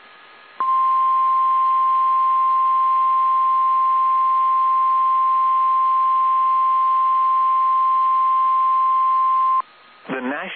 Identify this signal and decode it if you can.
Emergency Alert System (SAME) tones (the 1050 Hz attention tone), with speech. It is heard from 0.5 to 9.7 s.